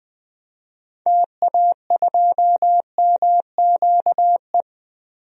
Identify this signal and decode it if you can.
Morse code, heard as a keyed tone, about 20 wpm, 700 Hz. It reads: TA2MQE
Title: DXCC-100 Call Signs - 6 Characters - 20wpm